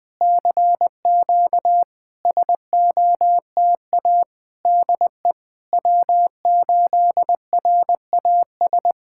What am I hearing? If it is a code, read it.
Morse code, heard as a keyed tone, about 20 wpm, 700 Hz. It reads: CQ SOTA DE W8RAS
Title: SOTA QSO Practice 20wpm